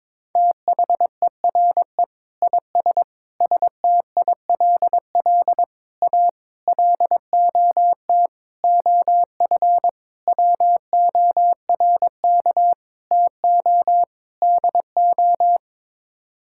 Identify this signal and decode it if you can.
Morse code, heard as a keyed tone, about 22 wpm, 700 Hz. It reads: THERE IS STILL A LOT OF WORK TO DO
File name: Sets of 9 Words - 22wpm